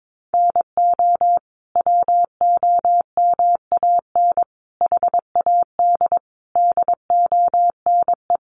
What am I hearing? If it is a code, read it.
Morse code, heard as a keyed tone, about 22 wpm, 700 Hz. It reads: NO WOMAN HAD DONE
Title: Sentences from Top 1100 Words 22wpm